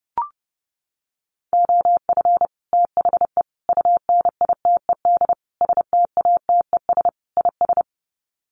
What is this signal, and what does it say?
Morse code, heard as a keyed tone, about 30 wpm, 700 Hz. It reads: OF THE UNITED STATES IS
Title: Sets of 5 Words Encore2 30wpm